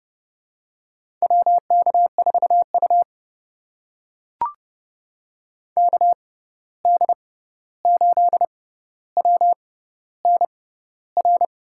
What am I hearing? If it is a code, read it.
Morse code, heard as a keyed tone, about 30 wpm, spaced as for 14 wpm, 700 Hz. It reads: WK4U KD8WNR
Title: USA Call Signs - 6x Character Spacing - 30wpm